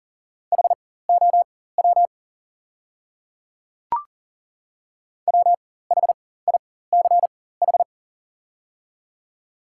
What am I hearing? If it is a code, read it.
Morse code, heard as a keyed tone, about 38 wpm, spaced as for 18 wpm, 690 Hz. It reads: HOW WHICH
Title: 4x Character Spacing - Top 100 Words - 40wpm